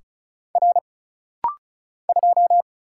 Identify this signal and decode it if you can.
Morse code, heard as a keyed tone, about 36 wpm, 700 Hz. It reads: R 2
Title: Single Letter-Number Alt - Rapid-Fire 35wpm